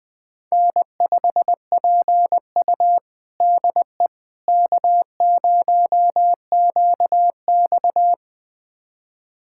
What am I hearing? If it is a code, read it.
Morse code, heard as a keyed tone, about 20 wpm, 700 Hz. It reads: N5PU DE K0QX
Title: USA Callsign DE Callsign - 4 Characters - 20wpm